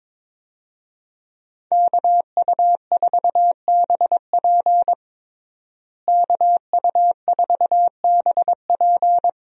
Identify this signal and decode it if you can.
Morse code, heard as a keyed tone, about 22 wpm, 700 Hz. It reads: KU4BP KU4BP
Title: USA Call Signs - 5 Characters - Repeated 2x - 22wpm